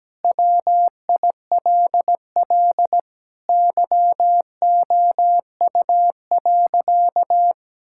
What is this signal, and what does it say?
Morse code, heard as a keyed tone, about 17 wpm, 700 Hz. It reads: WILL YOU.
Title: Sentences from Top 1000 Words - Review Part 6 - 17wpm